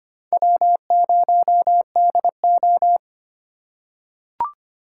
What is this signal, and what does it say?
Morse code, heard as a keyed tone, about 25 wpm, 700 Hz. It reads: W0DO
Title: USA Call Signs - 4 Characters - 4x Character Spacing - 25wpm